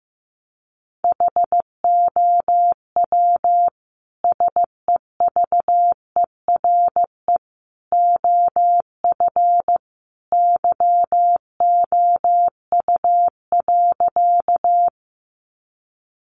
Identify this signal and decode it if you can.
Morse code, heard as a keyed tone, about 15 wpm, 700 Hz. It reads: HOW SEVERE OF YOU.
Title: Sentences from Top 1400 Words 15wpm